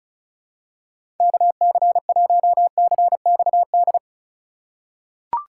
Morse code, heard as a keyed tone, about 34 wpm, 700 Hz. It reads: KC1CXD
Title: USA Call Signs - 6 Characters - 4x Character Spacing - 35wpm